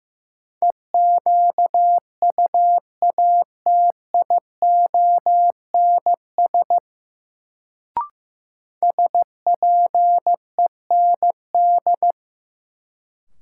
Morse code, heard as a keyed tone, about 16 wpm, 700 Hz. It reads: EQUATIONS SPEND
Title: Sets of 5 Words - New Word Drill - 15wpm